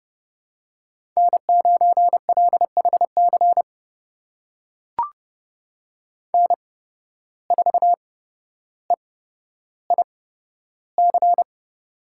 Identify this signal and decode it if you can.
Morse code, heard as a keyed tone, about 29 wpm, spaced as for 13 wpm, 700 Hz. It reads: N9LHC N4EIC
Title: USA Call Signs - 5 Characters - 8x Character Spacing - 30wpm